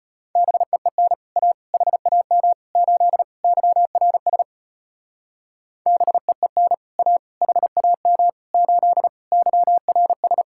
Morse code, heard as a keyed tone, about 33 wpm, 700 Hz. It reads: BEEN A HAM 8 YRS BEEN A HAM 8 YRS